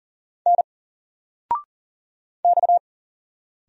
Morse code, heard as a keyed tone, about 39 wpm, 700 Hz. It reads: N X
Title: Single Letter-Number 40wpm